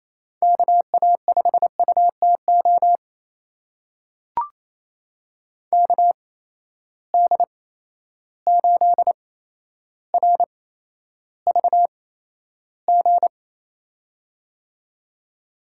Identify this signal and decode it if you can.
Morse code, heard as a keyed tone, about 28 wpm, spaced as for 12 wpm, 700 Hz. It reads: KA5UTO KD8RVG
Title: USA Call Signs - 6 Characters - 8x Character Spacing - 28wpm